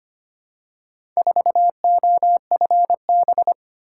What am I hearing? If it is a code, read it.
Morse code, heard as a keyed tone, about 25 wpm, 700 Hz. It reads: 4OFB
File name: practice / 25wpm / Four Letter-Number Combinations - 25wpm